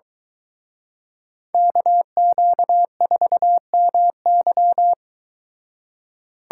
Morse code, heard as a keyed tone, about 23 wpm, 700 Hz. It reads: KQ4MY